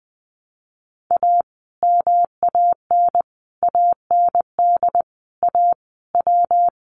Morse code, heard as a keyed tone, about 19 wpm, 700 Hz. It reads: A MAN AND A W
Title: Sets of 5 Words Encore 20wpm